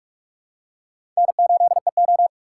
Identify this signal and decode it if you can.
Morse code, heard as a keyed tone, about 46 wpm, 700 Hz. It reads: N8EO
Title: USA Call Signs - 6x Character Spacing - 45wpm